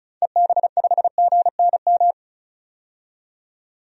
Morse code, heard as a keyed tone, about 34 wpm, 700 Hz. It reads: EB5GNM
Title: DXCC-100 Call Signs - 6 Characters - 35wpm